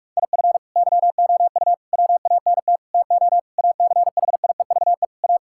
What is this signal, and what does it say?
Morse code, heard as a keyed tone, about 46 wpm, 700 Hz. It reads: IF YOU WANT TO ACHIEVE A
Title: Sentences from Top 1000 Words - Review Part 6 - 45wpm